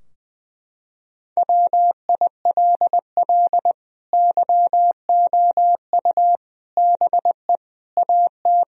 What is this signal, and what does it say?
Morse code, heard as a keyed tone, about 20 wpm, 700 Hz. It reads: WILL YOU BE AT
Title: Sentences from Top 1000 Words - Review Part 3 - 20wpm